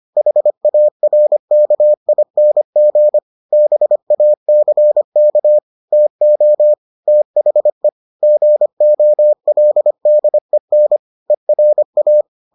Morse code, heard as a keyed tone, about 25 wpm, 600 Hz. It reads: HARKING BACK TO THE GOLDEN ERA